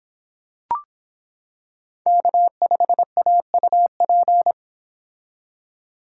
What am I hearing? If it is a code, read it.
Morse code, heard as a keyed tone, about 25 wpm, 700 Hz. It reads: K5AUP